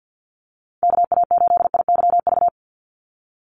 Morse code, heard as a keyed tone, about 50 wpm, 700 Hz. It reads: KA8IYV
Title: USA Call Signs 50wpm